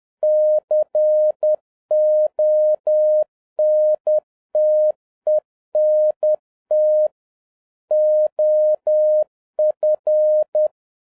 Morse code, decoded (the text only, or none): CONTENT OF